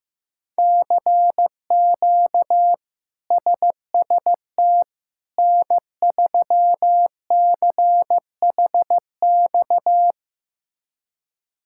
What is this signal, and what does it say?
Morse code, heard as a keyed tone, about 16 wpm, 700 Hz. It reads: CQ SST N3CHX
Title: K1USN SST QSO Practice - 15wpm